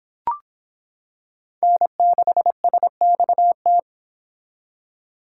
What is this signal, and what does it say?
Morse code, heard as a keyed tone, about 26 wpm, 700 Hz. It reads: N6SXT